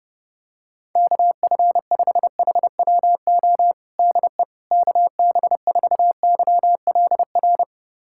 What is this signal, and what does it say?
Morse code, heard as a keyed tone, about 32 wpm, 700 Hz. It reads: KF5HWO DE KB4YLR
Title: USA Callsign DE Callsign - 6 Characters - 30wpm